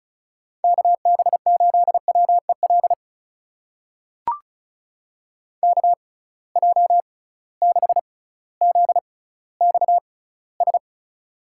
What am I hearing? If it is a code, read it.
Morse code, heard as a keyed tone, about 36 wpm, spaced as for 18 wpm, 700 Hz. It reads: KB8WEL KJ6ZXS
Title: USA Call Signs - 6 Characters - 6x Character Spacing - 35wpm